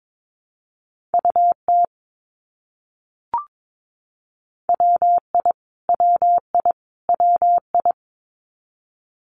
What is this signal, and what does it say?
Morse code, heard as a keyed tone, about 22 wpm, 700 Hz. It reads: UT WI WI WI
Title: US States Repeated 3x 22wpm